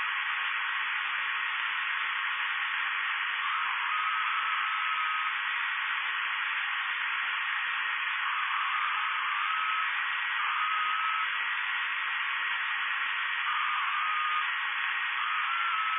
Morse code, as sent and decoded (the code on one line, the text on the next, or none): none